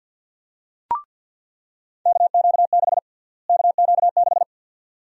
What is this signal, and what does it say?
Morse code, heard as a keyed tone, about 49 wpm, 700 Hz. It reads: KQ6 KQ6
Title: USA Call Sign Prefixes - Repeated 2x - 50wpm